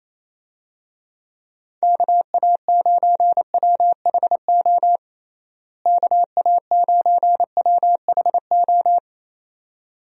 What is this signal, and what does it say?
Morse code, heard as a keyed tone, about 27 wpm, 700 Hz. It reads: KA9WHO KA9WHO
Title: USA Call Signs - 6 Characters - Repeated 2x - 28wpm